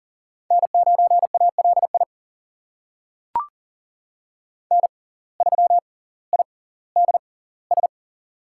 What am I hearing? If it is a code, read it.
Morse code, heard as a keyed tone, about 40 wpm, spaced as for 20 wpm, 700 Hz. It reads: N9ALI N3IDS
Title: USA Call Signs - 5 Characters - 6x Character Spacing - 40wpm